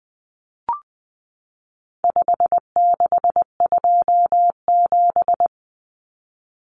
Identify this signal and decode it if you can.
Morse code, heard as a keyed tone, about 20 wpm, 700 Hz. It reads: 5627
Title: Common Cut Numbers - 4-Digit 20wpm